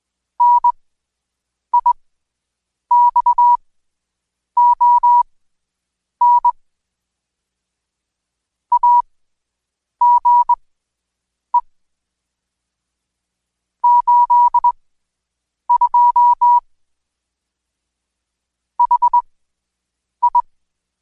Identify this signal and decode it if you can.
Morse code, heard as a keyed tone, about 21 wpm, spaced as for 7 wpm, 970 Hz. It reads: NIXON AGE 82 HI
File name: Code Quick / AudioDownld2 / Seg113-Day18-DayDrill 18